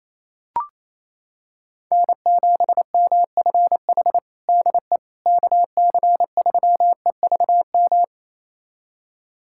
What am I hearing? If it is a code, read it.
Morse code, heard as a keyed tone, about 28 wpm, 700 Hz. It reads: N7MFH DE KC3EVM
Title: USA Callsign DE Callsign 28wpm